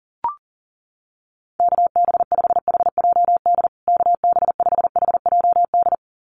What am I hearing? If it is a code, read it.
Morse code, heard as a keyed tone, about 40 wpm, 700 Hz. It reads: KB5HJD KB5HJD
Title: USA Call Signs Repeated 2x 40wpm